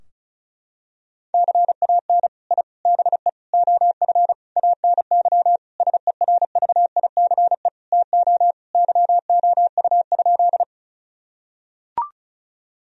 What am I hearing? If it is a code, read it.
Morse code, heard as a keyed tone, about 36 wpm, 700 Hz. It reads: CAN I BE OF ANY SERVICE TO YOU?
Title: Sentences from Top 1000 Words - Review Part 4 - 35wpm